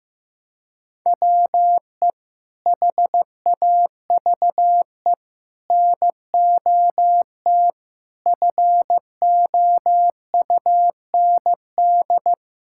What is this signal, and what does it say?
Morse code, heard as a keyed tone, about 15 wpm, 700 Hz. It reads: WE HAVE NOT FOUND
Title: Sentences from Top 1000 Words - Review Part 6 - 15wpm